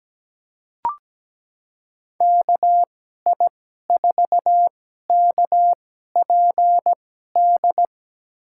Morse code, heard as a keyed tone, about 17 wpm, spaced as for 12 wpm, 700 Hz. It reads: KI4KPD
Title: USA Call Signs - 6 Characters - 2x Character Spacing - 17wpm